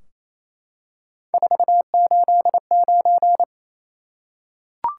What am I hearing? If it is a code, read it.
Morse code, heard as a keyed tone, about 28 wpm, 700 Hz. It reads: 489
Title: Top 180 QSO Elements 28wpm